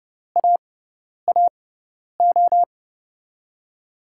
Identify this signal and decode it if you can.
Morse code, heard as a keyed tone, about 31 wpm, spaced as for 10 wpm, 700 Hz. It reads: AAO
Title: USA Call Signs - 5 Characters - 6x Character Spacing - 30wpm